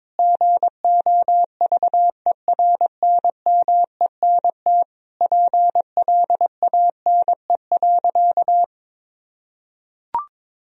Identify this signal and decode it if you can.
Morse code, heard as a keyed tone, about 21 wpm, 700 Hz. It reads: GOVERNMENT PLANE.
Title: Sentences from Top 1000 Words - Review Part 3 - 22wpm